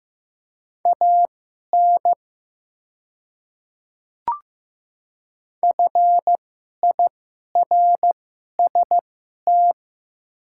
Morse code, heard as a keyed tone, about 15 wpm, spaced as for 11 wpm, 700 Hz. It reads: AN FIRST
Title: 2x Character Spacing - Top 200 Words - 15wpm